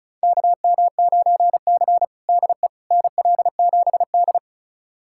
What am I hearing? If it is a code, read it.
Morse code, heard as a keyed tone, about 36 wpm, 700 Hz. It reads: KM9C DE NL7D
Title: USA Callsign DE Callsign - 4 Characters - 35wpm